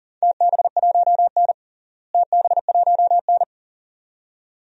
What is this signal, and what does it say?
Morse code, heard as a keyed tone, about 40 wpm, 700 Hz. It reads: TB1N TB1N
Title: DXCC-100 Call Signs - 4 Characters - Repeated 2x - 40wpm